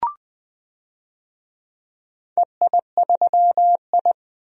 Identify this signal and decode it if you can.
Morse code, heard as a keyed tone, about 20 wpm, 700 Hz. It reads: EI3I
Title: DXCC-100 Call Signs - 4 Characters - 20wpm